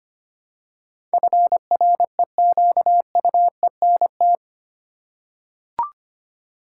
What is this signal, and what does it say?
Morse code, heard as a keyed tone, about 25 wpm, 700 Hz. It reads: FREQUENT
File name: 8 Letter Words - First 3 Letters Given - 25wpm